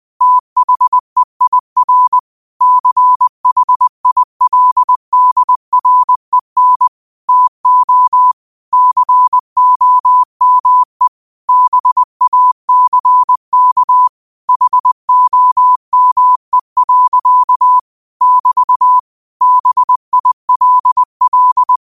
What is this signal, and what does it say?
Morse code, heard as a keyed tone, about 20 wpm, 1000 Hz. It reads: THEIR CHILDREN TO COME BACK HOME. <BT> BILL